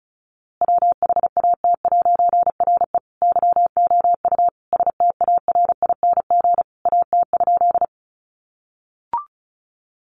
Morse code, heard as a keyed tone, about 34 wpm, 700 Hz. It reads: WHAT'RE YOU STARING AT?
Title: A Sound Of Thunder 35wpm